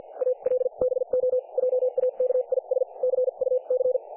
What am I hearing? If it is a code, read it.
Morse code, heard as a keyed tone, about 50 wpm, spaced as for 41 wpm, 500 Hz, with radio noise. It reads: ALBO JAKIS CUC
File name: dlazaawansowanych / nagraniazpasma / websdr recording 3564.7kHz